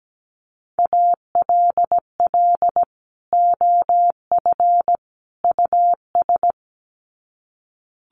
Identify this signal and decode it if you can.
Morse code, heard as a keyed tone, about 17 wpm, 700 Hz. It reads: ALL OF US
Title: Sets of 3 Words 17wpm